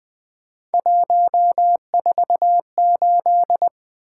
Morse code, heard as a keyed tone, about 21 wpm, 700 Hz. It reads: 148
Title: Three Letter-Number Combinations - 20wpm